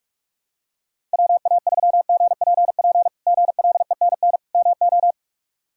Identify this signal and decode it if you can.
Morse code, heard as a keyed tone, about 43 wpm, 700 Hz. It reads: WA3GPP GLENN MO
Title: K1USN SST QSO Practice - 45wpm